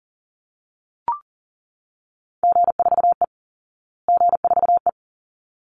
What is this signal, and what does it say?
Morse code, heard as a keyed tone, about 37 wpm, 700 Hz. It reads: G4E G4E
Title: Call Signs Repeated 3x - 40wpm